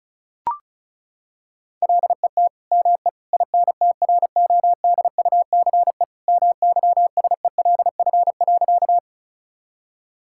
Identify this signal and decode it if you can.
Morse code, heard as a keyed tone, about 36 wpm, 700 Hz. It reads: LET ME INTRODUCE MYSELF.
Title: Sentences from Top 1600 Words 35wpm